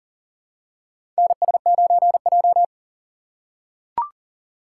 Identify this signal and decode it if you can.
Morse code, heard as a keyed tone, about 40 wpm, 700 Hz. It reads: NS9J
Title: USA Call Signs - 6x Character Spacing - 40wpm